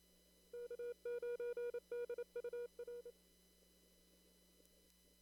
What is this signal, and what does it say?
Morse code, heard as a keyed tone, about 32 wpm, 480 Hz. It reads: K9DUR
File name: CW / kt5j wpx cw-op-k5pi